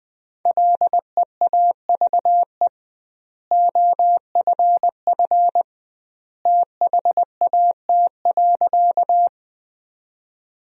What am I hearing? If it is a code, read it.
Morse code, heard as a keyed tone, about 20 wpm, 700 Hz. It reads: LEAVE OFF THAT.
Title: 2x Word-Spacing - Sentences from Top 400 Words - 20wpm